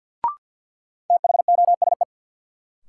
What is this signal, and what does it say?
Morse code, heard as a keyed tone, about 46 wpm, 700 Hz. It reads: THOSE
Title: Sentences from Top 300 Words - New Word Drill - 50wpm